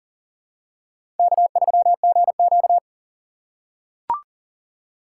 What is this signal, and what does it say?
Morse code, heard as a keyed tone, about 40 wpm, 700 Hz. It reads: K3GQ
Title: USA Call Signs - 4 Characters - 2x Character Spacing - 40wpm